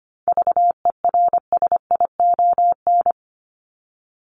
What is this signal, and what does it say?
Morse code, heard as a keyed tone, about 25 wpm, 700 Hz. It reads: VERSION
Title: Top 1251-1500 Words 25wpm